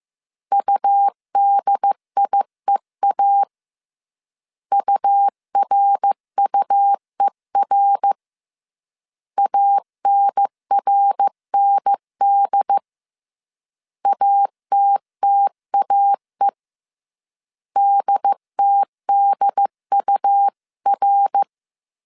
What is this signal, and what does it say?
Morse code, heard as a keyed tone, about 15 wpm, 800 Hz. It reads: UDIEA URUER ANRND ATTAE DTDUR
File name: CW / DET / 1 3 1500C S C 14wpm 2458